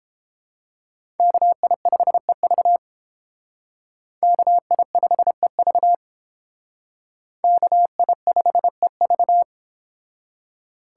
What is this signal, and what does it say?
Morse code, heard as a keyed tone, about 31 wpm, 700 Hz. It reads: KI5EV KI5EV KI5EV